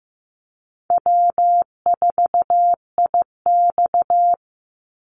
Morse code, heard as a keyed tone, about 15 wpm, 700 Hz. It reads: W4IX
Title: Easy Call Signs Repeated 3x - 15wpm